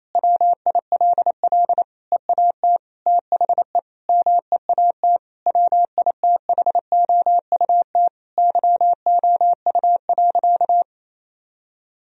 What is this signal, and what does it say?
Morse code, heard as a keyed tone, about 27 wpm, 700 Hz. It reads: WILL EAT THE MEAT WITHOUT YOU.
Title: Sentences from Top 1000 Words - Review Part 3 - 28wpm